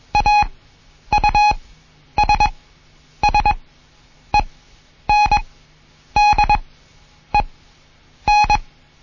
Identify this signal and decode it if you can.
Morse code, heard as a keyed tone, about 21 wpm, spaced as for 9 wpm, 840 Hz. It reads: AUSSENDEN
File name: Morsen / Kurs A 20